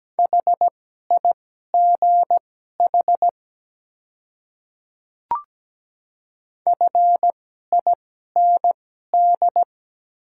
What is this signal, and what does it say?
Morse code, heard as a keyed tone, about 17 wpm, spaced as for 12 wpm, 700 Hz. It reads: HIGH FIND
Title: 2x Character Spacing - Top 200 Words - 17wpm